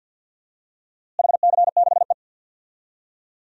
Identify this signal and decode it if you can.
Morse code, heard as a keyed tone, about 47 wpm, 700 Hz. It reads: HC6E
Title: DXCC-100 Call Signs - 4 Characters - Repeated 3x - 50wpm